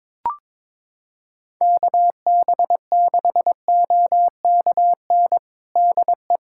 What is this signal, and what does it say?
Morse code, heard as a keyed tone, about 22 wpm, 700 Hz. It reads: KB6OKN DE
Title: USA Callsign DE Callsign - 6 Characters - 22wpm